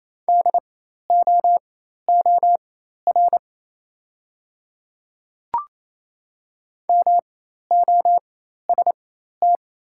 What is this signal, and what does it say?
Morse code, heard as a keyed tone, about 28 wpm, spaced as for 13 wpm, 700 Hz. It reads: DOOR MOST
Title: 4x Character Spacing - Top 300 Words - 28wpm